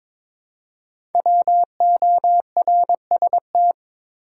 Morse code, heard as a keyed tone, about 22 wpm, 700 Hz. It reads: WORST
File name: 5 Letter Words - First 3 Letters Given - 22wpm